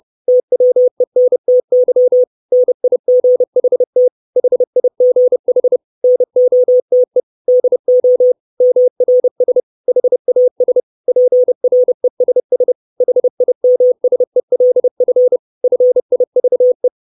Morse code, heard as a keyed tone, about 30 wpm, 500 Hz. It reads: TWENTY NIGHT HIGH NOTE DO MRS HAS PRESS HIMSELF FIVE
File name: 632 Words 30WPM